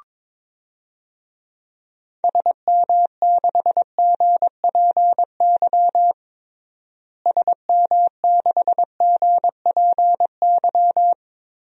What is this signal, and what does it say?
Morse code, heard as a keyed tone, about 22 wpm, 700 Hz. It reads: SM6GPY SM6GPY
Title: DXCC-100 Call Signs - 6 Characters - Repeated 3x - 22wpm